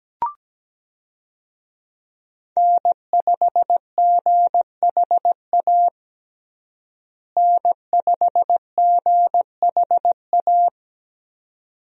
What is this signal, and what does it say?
Morse code, heard as a keyed tone, about 17 wpm, 700 Hz. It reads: N5GHA N5GHA
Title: USA Call Signs - 5 Characters - Repeated 2x - 17wpm